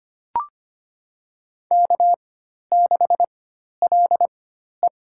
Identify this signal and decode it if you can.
Morse code, heard as a keyed tone, about 25 wpm, spaced as for 12 wpm, 700 Hz. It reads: K6LE
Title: USA Call Signs - 4 Characters - 4x Character Spacing - 25wpm